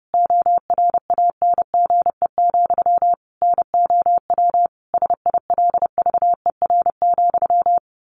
Morse code, heard as a keyed tone, about 30 wpm, 700 Hz. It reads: ORANGE, NOW SILVER,
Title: A Sound Of Thunder 30wpm